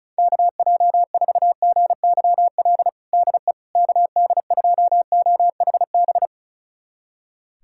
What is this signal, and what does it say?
Morse code, heard as a keyed tone, about 34 wpm, 700 Hz. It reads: KJ4GYL DE KD2OHB
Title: USA Callsign DE Callsign - 6 Characters - 35wpm